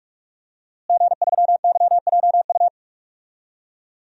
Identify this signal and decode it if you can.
Morse code, heard as a keyed tone, about 46 wpm, 700 Hz. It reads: G3YJU
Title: DXCC-100 Call Signs - 5 Characters - Repeated 3x - 45wpm